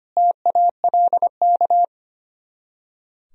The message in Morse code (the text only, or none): TALK